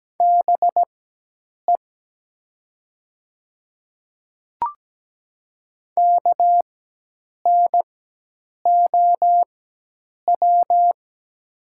Morse code, heard as a keyed tone, about 17 wpm, spaced as for 8 wpm, 690 Hz. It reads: BE KNOW
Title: 4x Character Spacing - Top 100 Words - 17wpm